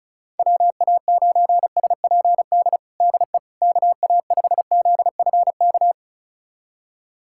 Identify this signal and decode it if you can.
Morse code, heard as a keyed tone, about 36 wpm, 700 Hz. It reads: WA9SPD DE KA5ZFK
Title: USA Callsign DE Callsign - 6 Characters - 35wpm